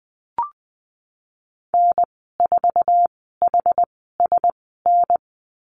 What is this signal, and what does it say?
Morse code, heard as a keyed tone, about 19 wpm, spaced as for 14 wpm, 700 Hz. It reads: N4HSN
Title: 076 Lesson 13g - Letter L - Call Signs 20wpm